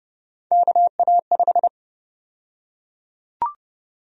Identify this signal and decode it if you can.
Morse code, heard as a keyed tone, about 32 wpm, 700 Hz. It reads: KA5
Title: USA Call Sign Prefixes - Repeated 2x - 30wpm